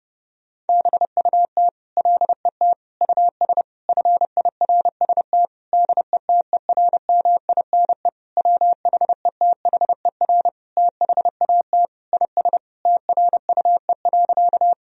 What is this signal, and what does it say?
Morse code, heard as a keyed tone, about 30 wpm, 700 Hz. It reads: BUT LET US FIRST DETERMINE WHETHER THAT IS TRUE.
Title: Sentences from Top 1000 Words - Review Part 4 - 30wpm